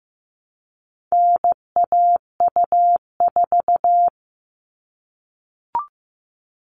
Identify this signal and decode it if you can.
Morse code, heard as a keyed tone, about 15 wpm, 700 Hz. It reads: NAU4
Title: All Cut Numbers - 4-Digit 15wpm